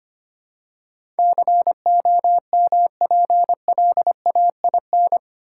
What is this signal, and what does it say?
Morse code, heard as a keyed tone, about 25 wpm, 700 Hz. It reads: COMPLAIN
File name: 8 Letter Words - First 3 Letters Given - 25wpm